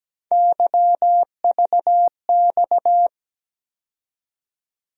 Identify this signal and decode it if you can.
Morse code, heard as a keyed tone, about 17 wpm, 700 Hz. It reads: YVX
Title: USA Call Sign Postfixes - 3 Letters 17wpm